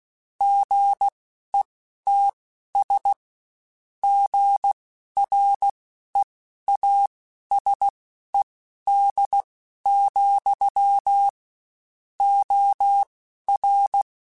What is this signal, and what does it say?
Morse code, heard as a keyed tone, about 16 wpm, spaced as for 12 wpm, 800 Hz. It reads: GETS GREASED, OR
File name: Spaced / Chuckles Spaced-mp3s / Squeaky sp @15